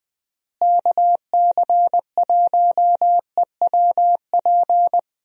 Morse code, heard as a keyed tone, about 20 wpm, 700 Hz. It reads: KC1EWP
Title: USA Call Signs - 6 Characters - Repeated 2x - 20wpm